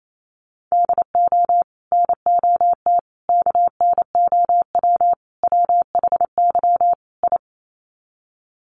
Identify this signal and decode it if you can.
Morse code, heard as a keyed tone, about 28 wpm, 700 Hz. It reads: DO NOT KNOW WHY I
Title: Sets of 5 Words Encore 28wpm